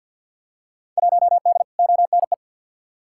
Morse code, heard as a keyed tone, about 51 wpm, 700 Hz. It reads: 1D ONE
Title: Field Day Exchanges - 50wpm